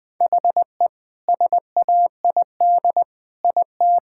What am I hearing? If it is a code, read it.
Morse code, heard as a keyed tone, about 20 wpm, 700 Hz. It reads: HE SAID IT
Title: Sentences with 2-Word Phrases - Spoken Beginning - 20wpm